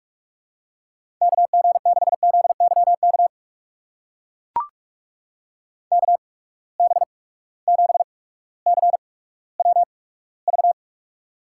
Morse code, heard as a keyed tone, about 44 wpm, spaced as for 19 wpm, 700 Hz. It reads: KG6ZYK KB7CWV
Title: USA Call Signs - 6 Characters - 8x Character Spacing - 45wpm